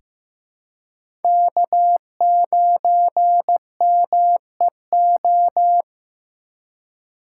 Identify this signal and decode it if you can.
Morse code, heard as a keyed tone, about 15 wpm, 700 Hz. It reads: K9MEO
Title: USA Call Signs - 4x Character Spacing - 15wpm